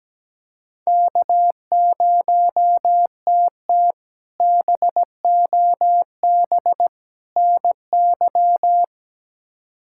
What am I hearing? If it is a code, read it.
Morse code, heard as a keyed tone, about 17 wpm, 700 Hz. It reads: K0TT BOB NY
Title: K1USN SST QSO Practice - 17wpm